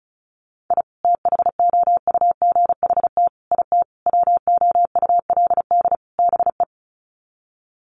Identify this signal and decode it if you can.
Morse code, heard as a keyed tone, about 34 wpm, 700 Hz. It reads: I THOUGHT IT WOULD BE
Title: Sets of 5 Words Encore 35wpm